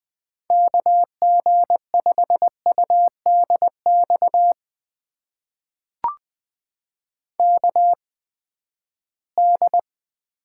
Morse code, heard as a keyed tone, about 20 wpm, spaced as for 14 wpm, 700 Hz. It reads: KG5UDX KD
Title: USA Call Signs - 6 Characters - 8x Character Spacing - 20wpm